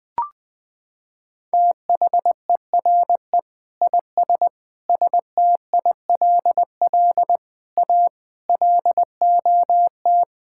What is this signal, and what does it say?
Morse code, heard as a keyed tone, about 21 wpm, 700 Hz. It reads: THERE IS STILL A LOT
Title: Sentences with 9-Word Phrases - 20wpm